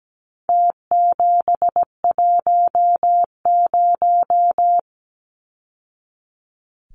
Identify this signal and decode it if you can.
Morse code, heard as a keyed tone, about 17 wpm, 700 Hz. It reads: T710
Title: Numbers - Contest Style - 4-Digit 17wpm